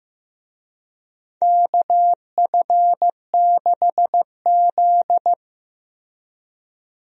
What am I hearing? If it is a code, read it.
Morse code, heard as a keyed tone, about 15 wpm, 700 Hz. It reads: KF6Z
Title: USA Call Signs - 4 Characters - Repeated 2x - 15wpm